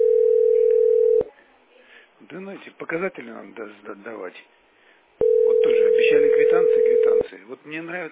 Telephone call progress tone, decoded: ringback tone

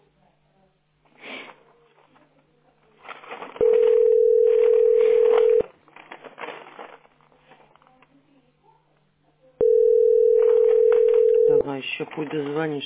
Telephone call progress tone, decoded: ringback tone

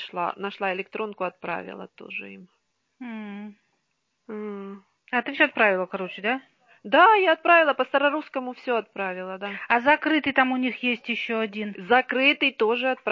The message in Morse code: none

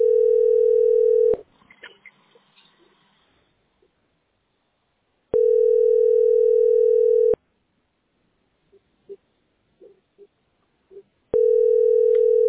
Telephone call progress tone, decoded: ringback tone

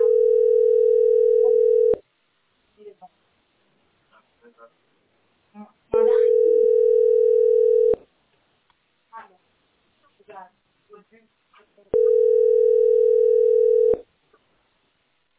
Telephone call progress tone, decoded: ringback tone

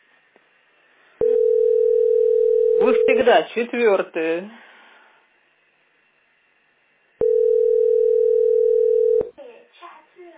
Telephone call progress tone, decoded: ringback tone

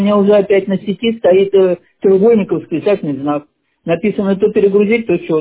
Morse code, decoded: none

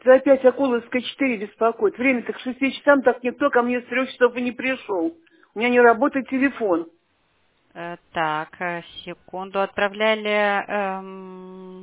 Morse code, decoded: none

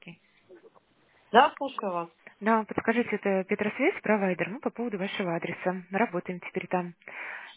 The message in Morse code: none

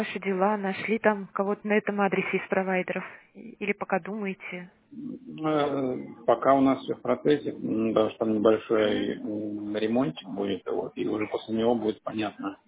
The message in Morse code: none

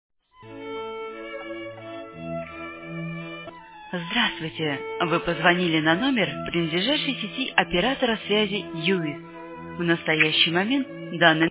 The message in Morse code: none

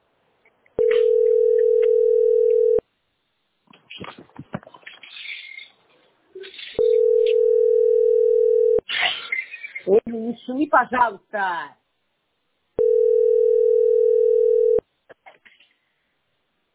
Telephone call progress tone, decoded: ringback tone

